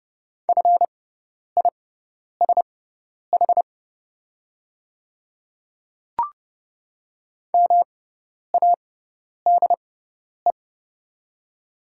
Morse code, heard as a keyed tone, about 30 wpm, spaced as for 10 wpm, 690 Hz. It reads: FISH MADE
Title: 6x Character Spacing - Top 500 Words - 30wpm